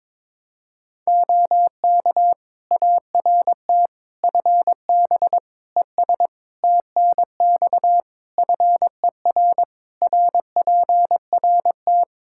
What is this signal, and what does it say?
Morse code, heard as a keyed tone, about 22 wpm, 700 Hz. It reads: OK ART FB ES TNX FER RPRT
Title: Practice QSO - Exchange 2 - Info - 22wpm